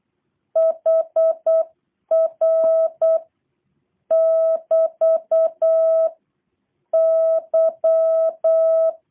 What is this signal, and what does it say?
Morse code, heard as a keyed tone, about 8 wpm, 660 Hz. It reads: HR <BT> Y
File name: jh8cbh / 2023cw / review1-6